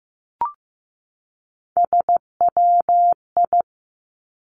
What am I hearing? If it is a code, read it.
Morse code, heard as a keyed tone, about 15 wpm, 700 Hz. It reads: SWI
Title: Three Challenging Letter-Number Combinations 15wpm